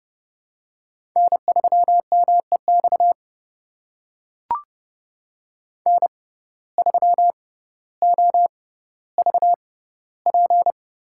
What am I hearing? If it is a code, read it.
Morse code, heard as a keyed tone, about 28 wpm, spaced as for 15 wpm, 700 Hz. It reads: N3MEX N3OVP
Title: USA Call Signs - 5 Characters - 6x Character Spacing - 30wpm